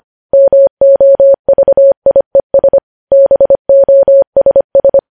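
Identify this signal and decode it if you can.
Morse code, heard as a keyed tone, about 25 wpm, 550 Hz. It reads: MOVIES BOSS
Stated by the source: Morse code news headlines at 25 wpm.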